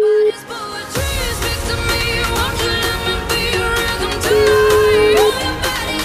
Ringback tone (Europe, 425 Hz), with noise.